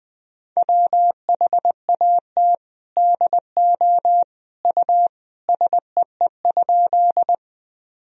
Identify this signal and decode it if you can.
Morse code, heard as a keyed tone, about 21 wpm, 700 Hz. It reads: WHAT DO U SEE?
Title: Sentences from Top 100 Words and Abbreviations 20wpm